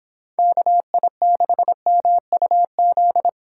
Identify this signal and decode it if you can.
Morse code, heard as a keyed tone, about 26 wpm, 700 Hz. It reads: KI6MUZ
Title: Speed Racing - USA Call Signs 22wpm